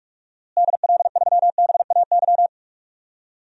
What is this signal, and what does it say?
Morse code, heard as a keyed tone, about 42 wpm, 700 Hz. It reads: DL3BAY
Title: DXCC-100 Call Signs - 6 Characters - Repeated 3x - 45wpm